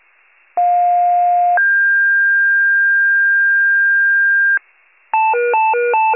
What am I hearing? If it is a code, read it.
Two-tone page, heard as a one-second tone, then a three-second tone, about 710 then 1600 Hz. It is heard from 0.6 to 4.6 s.